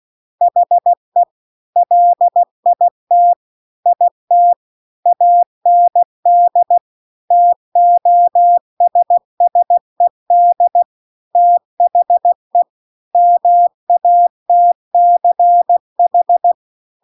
Morse code, heard as a keyed tone, about 16 wpm, 700 Hz. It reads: HE LIT IT AND TOSSED THE MATCH